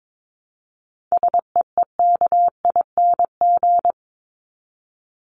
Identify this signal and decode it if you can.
Morse code, heard as a keyed tone, about 22 wpm, 700 Hz. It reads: SEEKING
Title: ING Suffix - Top 100 Words with Base Words 22wpm